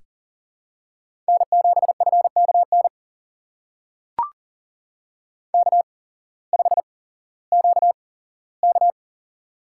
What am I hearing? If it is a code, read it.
Morse code, heard as a keyed tone, about 39 wpm, spaced as for 19 wpm, 700 Hz. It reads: N7FKN K5QK
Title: USA Call Signs - 8x Character Spacing - 40wpm